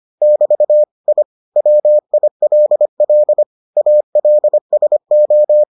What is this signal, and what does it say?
Morse code, heard as a keyed tone, about 25 wpm, 600 Hz. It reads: <BT> I WILL ALSO